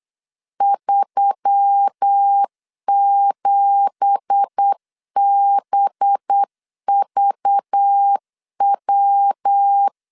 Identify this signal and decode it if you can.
Morse code, heard as a keyed tone, about 8 wpm, 800 Hz. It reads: 37BVW
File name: 8wpm